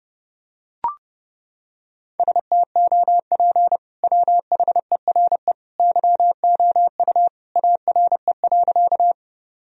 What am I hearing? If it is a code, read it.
Morse code, heard as a keyed tone, about 32 wpm, 700 Hz. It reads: STOP WHERE YOU ARE.
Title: Sentences from Top 1000 Words - Review Part 1 - 30wpm